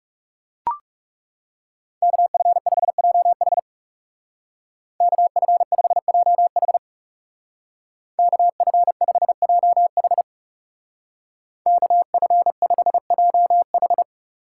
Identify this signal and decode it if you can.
Morse code, heard as a keyed tone, about 37 wpm, 700 Hz. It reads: KF5JH KF5JH KF5JH KF5JH
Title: Speed Racing - USA 2x2 Call Signs 30wpm